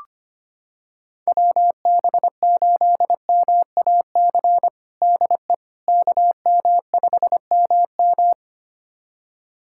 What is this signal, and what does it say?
Morse code, heard as a keyed tone, about 25 wpm, 700 Hz. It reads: WB8MAC DE KM5MM
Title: USA Callsign DE Callsign 25wpm